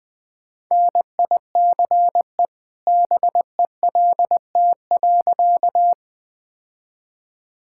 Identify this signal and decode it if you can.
Morse code, heard as a keyed tone, about 20 wpm, 700 Hz. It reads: NICE BELT.
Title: All-Age Appropriate Jokes - 20wpm